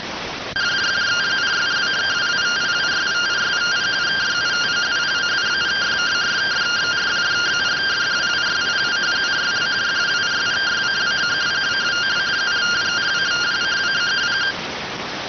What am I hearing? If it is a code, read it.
Two-tone teleprinter signal, about 170 Hz shift.